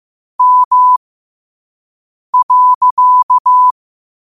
Morse code, heard as a keyed tone, about 15 wpm, spaced as for 5 wpm, 1000 Hz. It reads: M.
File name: qotdfiles / 5wpm